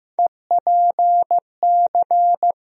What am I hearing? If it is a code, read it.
Morse code, heard as a keyed tone, about 16 wpm, 700 Hz. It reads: EPC
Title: USA Call Sign Postfixes - 3 Letters 15wpm